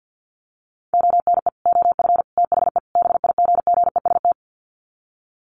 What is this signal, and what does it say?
Morse code, heard as a keyed tone, about 50 wpm, 700 Hz. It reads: ONE OF THE BIGGEST
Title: Sets of 4 Words 50wpm